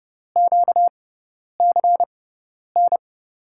Morse code, heard as a keyed tone, about 30 wpm, spaced as for 10 wpm, 700 Hz. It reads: QCN
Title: USA Call Signs - 6 Characters - 6x Character Spacing - 30wpm